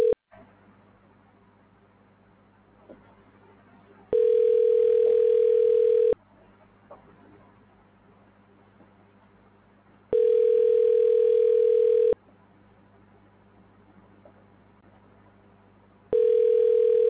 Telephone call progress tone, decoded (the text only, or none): ringback tone